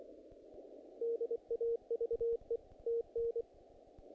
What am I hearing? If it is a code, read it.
Morse code, heard as a keyed tone, about 25 wpm, 460 Hz. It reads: DAVE TN
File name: contests / naqp-cw-k5tr-op-k5pi- -1900